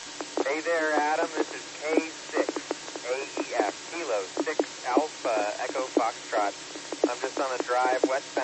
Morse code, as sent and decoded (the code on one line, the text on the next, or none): none